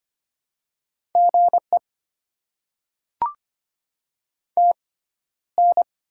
Morse code, heard as a keyed tone, about 25 wpm, spaced as for 13 wpm, 700 Hz. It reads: GE TN
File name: Top 20 QSO Elements - 6x Character Spacing - 25wpm